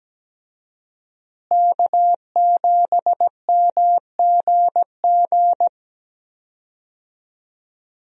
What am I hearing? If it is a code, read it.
Morse code, heard as a keyed tone, about 17 wpm, 700 Hz. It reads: K7MGG
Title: USA Call Signs - 5 Characters - Repeated 3x - 17wpm